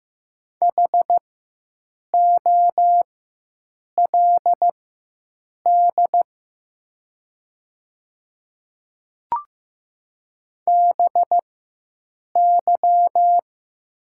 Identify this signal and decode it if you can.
Morse code, heard as a keyed tone, about 15 wpm, spaced as for 7 wpm, 700 Hz. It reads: HOLD BY
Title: 4x Character Spacing - Top 500 Words - 15wpm